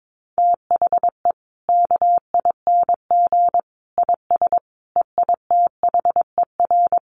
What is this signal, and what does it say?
Morse code, heard as a keyed tone, about 22 wpm, 700 Hz. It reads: THE KING IS EITHER
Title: Sentences from Top 600 Words 22wpm